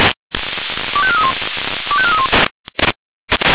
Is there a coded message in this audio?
Yes: five-tone selective call (ZVEI) from 1.0 to 2.3 s, heard as quick runs of five tones, tones of about 70 ms. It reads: 25431 25431